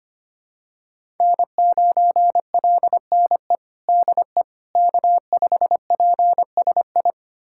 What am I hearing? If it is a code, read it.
Morse code, heard as a keyed tone, about 25 wpm, 700 Hz. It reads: N9LNE DE K5PSI
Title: USA Callsign DE Callsign - 5 Characters - 25wpm